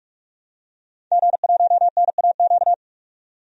Morse code, heard as a keyed tone, about 42 wpm, 700 Hz. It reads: G1NAQ
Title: DXCC-100 Call Signs - 5 Characters - 45wpm